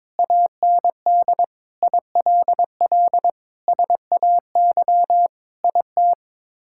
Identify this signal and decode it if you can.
Morse code, heard as a keyed tone, about 22 wpm, 700 Hz. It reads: AND ILL SAY IT
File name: practice / 22wpm / Sets of 8 Words - 22wpm